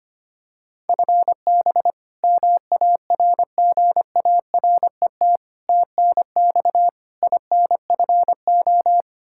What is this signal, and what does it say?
Morse code, heard as a keyed tone, about 25 wpm, 700 Hz. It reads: FB MARGARET TNX INFO
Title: Practice QSO - Exchange 3 - More Info - 25wpm